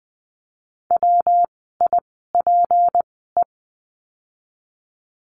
Morse code, heard as a keyed tone, about 20 wpm, spaced as for 15 wpm, 700 Hz. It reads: WIPE
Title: 153 Lesson 25f - Letter p - Words 20wpm